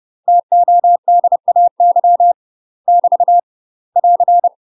Morse code, heard as a keyed tone, about 30 wpm, 700 Hz. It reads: TODAY <BT> <AR>
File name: UK MASTER QSOs TRAINING / 30WPM / UK MASTER QSOs 0040